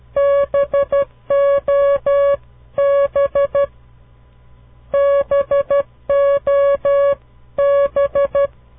Morse code, heard as a keyed tone, about 12 wpm, 560 Hz. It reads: BOB BOB
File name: CW Learning Programs / farnsworth / unit 2